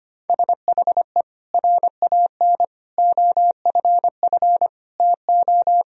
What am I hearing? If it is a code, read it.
Morse code, heard as a keyed tone, about 25 wpm, 700 Hz. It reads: SHE RAN OFF TO
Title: Sentences from Top 1000 Words - Review Part 3 - 25wpm